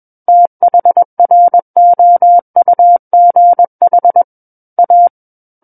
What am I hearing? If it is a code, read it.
Morse code, heard as a keyed tone, about 21 wpm, 700 Hz. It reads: THROUGH A
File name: Book One / 12-Chapter 12 - 21 WPM